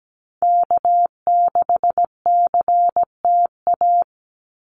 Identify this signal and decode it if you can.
Morse code, heard as a keyed tone, about 17 wpm, 700 Hz. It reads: K6CTA
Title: Call Signs Repeated 2x - 17wpm